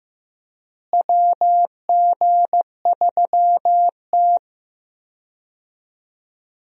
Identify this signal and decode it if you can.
Morse code, heard as a keyed tone, about 15 wpm, 700 Hz. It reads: WG3T
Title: USA Call Signs - 4 Characters - Repeated 2x - 15wpm